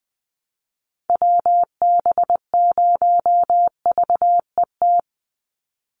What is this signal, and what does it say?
Morse code, heard as a keyed tone, about 20 wpm, 700 Hz. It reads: WB0VET
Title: USA Call Signs 20wpm